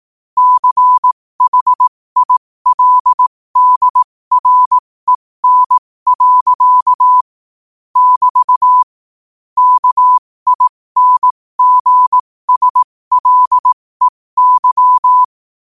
Morse code, heard as a keyed tone, about 18 wpm, spaced as for 15 wpm, 1000 Hz. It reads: CHILDREN. <BT> KINGSLEY
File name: qotdfiles / 15wpm